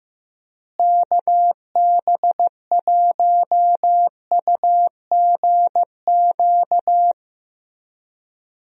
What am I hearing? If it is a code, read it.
Morse code, heard as a keyed tone, about 15 wpm, 700 Hz. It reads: KB1UGQ
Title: USA Call Signs - 6 Characters - Repeated 3x - 15wpm